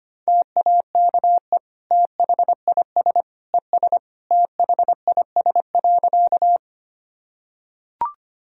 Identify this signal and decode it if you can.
Morse code, heard as a keyed tone, about 25 wpm, 700 Hz. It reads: TAKE THIS ES THIS.
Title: Sentences from Top 200 Words and Abbreviations 25wpm